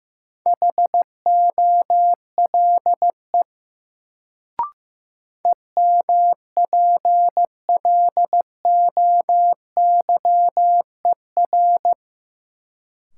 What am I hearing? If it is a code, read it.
Morse code, heard as a keyed tone, about 15 wpm, 700 Hz. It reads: HOLE EMPLOYER
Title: Sentences from Top 1600 Words - New Word Drill - 15wpm